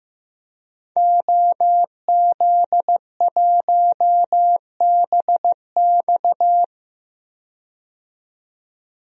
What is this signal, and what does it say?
Morse code, heard as a keyed tone, about 15 wpm, 700 Hz. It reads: OZ1BX
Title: DXCC-100 Call Signs - 5 Characters - 15wpm